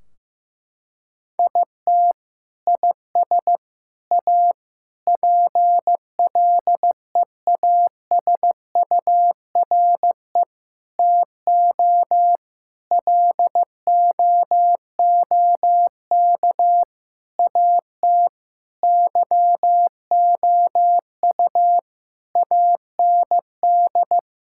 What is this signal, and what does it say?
Morse code, heard as a keyed tone, about 16 wpm, 700 Hz. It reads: IT IS A PLEASURE TO LOOK AT YOU AND
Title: Sentences from Top 1600 Words 15wpm